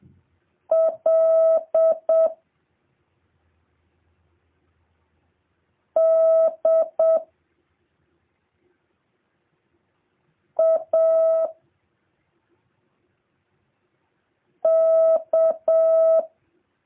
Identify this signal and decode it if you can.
Morse code, heard as a keyed tone, about 7 wpm, 660 Hz. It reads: L D A K